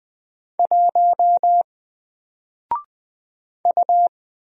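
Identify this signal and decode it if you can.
Morse code, heard as a keyed tone, about 20 wpm, 700 Hz. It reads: 1 U
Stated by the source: Single letter and number Morse drill at 20 wpm, from N4LCD.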